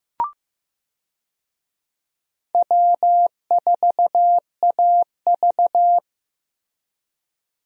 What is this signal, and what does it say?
Morse code, heard as a keyed tone, about 15 wpm, 700 Hz. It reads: W4AV
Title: USA Call Signs - 4 Characters - Repeated 3x - 15wpm